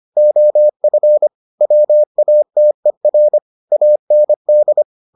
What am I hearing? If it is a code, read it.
Morse code, heard as a keyed tone, about 25 wpm, 600 Hz. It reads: OF WATER AND